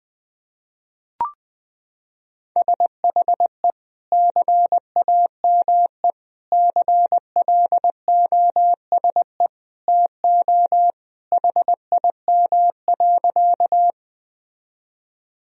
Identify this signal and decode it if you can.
Morse code, heard as a keyed tone, about 20 wpm, 700 Hz. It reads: SHE CAME CLOSE TO HIM.
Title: Sentences from Top 1000 Words - Review Part 1 - 20wpm